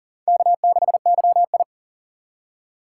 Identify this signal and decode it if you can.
Morse code, heard as a keyed tone, about 39 wpm, 700 Hz. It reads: K6YI
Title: USA Call Signs - 4 Characters - 2x Character Spacing - 40wpm